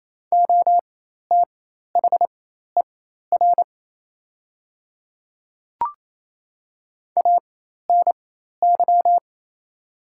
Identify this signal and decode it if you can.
Morse code, heard as a keyed tone, about 27 wpm, spaced as for 13 wpm, 700 Hz. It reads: OTHER ANY
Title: 4x Character Spacing - Top 300 Words - 28wpm